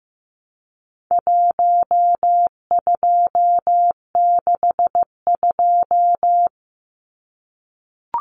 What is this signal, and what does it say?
Morse code, heard as a keyed tone, about 15 wpm, 700 Hz. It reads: 1262
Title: Numbers - 4-Digit 15wpm